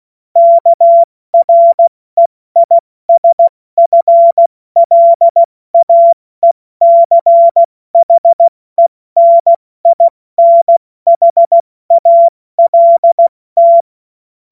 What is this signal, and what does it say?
Morse code, heard as a keyed tone, about 16 wpm, 680 Hz. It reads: KREISFLAECHENINHALT
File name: Morsekurs / Witze / Witz 21. Juni 2022 70 F80